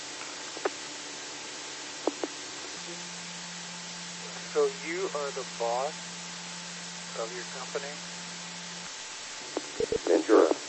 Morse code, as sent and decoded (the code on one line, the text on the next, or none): none